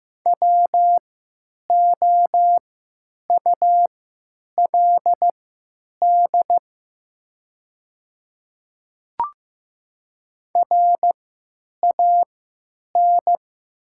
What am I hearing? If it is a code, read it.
Morse code, heard as a keyed tone, about 15 wpm, spaced as for 9 wpm, 700 Hz. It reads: WOULD RAN
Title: 3x Character Spacing - Top 500 Words - 15wpm